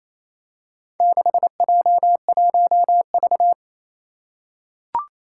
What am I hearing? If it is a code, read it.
Morse code, heard as a keyed tone, about 28 wpm, 700 Hz. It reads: 6J1V
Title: Four Challenging Letter-Number Combinations 28wpm